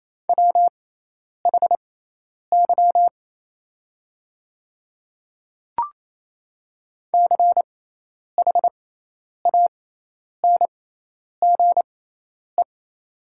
Morse code, heard as a keyed tone, about 28 wpm, spaced as for 10 wpm, 700 Hz. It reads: WHY CHANGE
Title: 6x Character Spacing - Top 200 Words - 28wpm